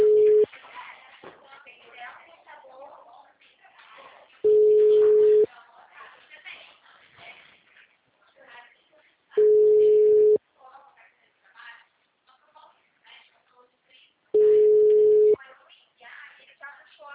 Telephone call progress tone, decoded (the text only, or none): ringback tone